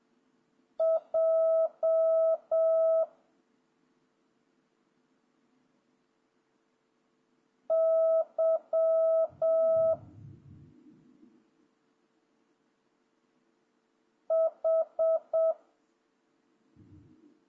Morse code, heard as a keyed tone, about 7 wpm, 650 Hz. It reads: J Y H
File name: jh8cbh / 2026cw